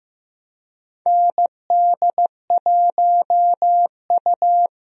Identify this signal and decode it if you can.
Morse code, heard as a keyed tone, about 15 wpm, 700 Hz. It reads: ND1U
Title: USA Call Signs - 4 Characters - Repeated 2x - 15wpm